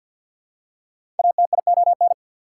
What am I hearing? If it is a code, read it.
Morse code, heard as a keyed tone, about 48 wpm, 700 Hz. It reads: ATION
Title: Prefixes and Suffixes 50wpm